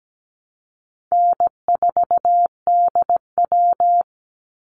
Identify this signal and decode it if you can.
Morse code, heard as a keyed tone, about 17 wpm, 700 Hz. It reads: N4DW
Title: Call Signs - 17wpm